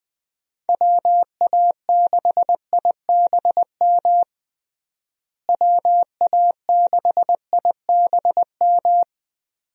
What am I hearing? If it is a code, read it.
Morse code, heard as a keyed tone, about 20 wpm, 700 Hz. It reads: WA6IBM WA6IBM
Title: USA Call Signs - 6 Characters - Repeated 2x - 20wpm